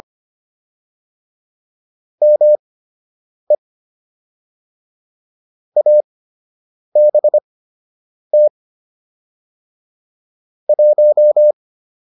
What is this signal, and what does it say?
Morse code, heard as a keyed tone, about 25 wpm, spaced as for 8 wpm, 600 Hz. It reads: ME ABT 1